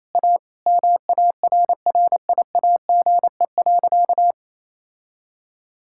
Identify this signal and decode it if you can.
Morse code, heard as a keyed tone, about 28 wpm, 700 Hz. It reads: A MARRIAGE.
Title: Sentences from Top 1000 Words - Review Part 5 - 28wpm